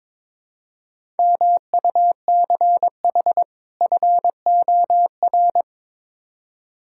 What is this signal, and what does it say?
Morse code, heard as a keyed tone, about 21 wpm, 700 Hz. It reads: MUCH FOR
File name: practice / 22wpm / Build A Phrase - 5-Word Phrases - 22wpm